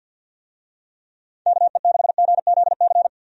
Morse code, heard as a keyed tone, about 52 wpm, 700 Hz. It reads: KE6GZC
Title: USA Call Signs - 6 Characters - 50wpm